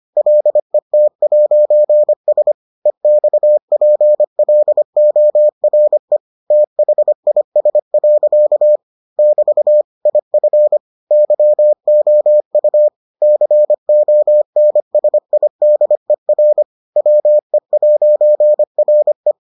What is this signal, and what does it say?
Morse code, heard as a keyed tone, about 25 wpm, 600 Hz. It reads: LET'S EXPLORE THIS. <BT> IF YOU CONSIDER WE'RE